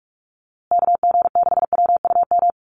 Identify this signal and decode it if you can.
Morse code, heard as a keyed tone, about 44 wpm, 700 Hz. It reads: KG6WUM
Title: USA Call Signs Repeated 3x 45wpm